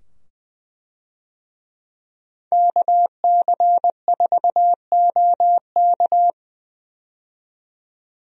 Morse code, heard as a keyed tone, about 20 wpm, 700 Hz. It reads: KC4OK
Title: USA Call Signs - 5 Characters - 20wpm